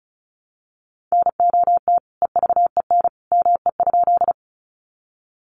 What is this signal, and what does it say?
Morse code, heard as a keyed tone, about 34 wpm, 700 Hz. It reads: NOT EVEN ME?
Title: Sentences from Top 600 Words 35wpm